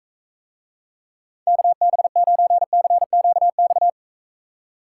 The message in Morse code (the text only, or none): KB9CQX